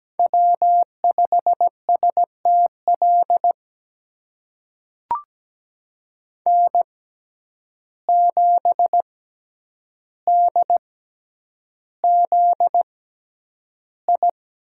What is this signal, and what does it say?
Morse code, heard as a keyed tone, about 17 wpm, spaced as for 9 wpm, 700 Hz. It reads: W5STL N7DZI
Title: USA Call Signs - 5 Characters - 6x Character Spacing - 17wpm